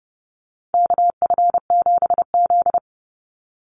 Morse code, heard as a keyed tone, about 30 wpm, 700 Hz. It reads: KF7Z
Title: Call Signs Repeated 3x - 30wpm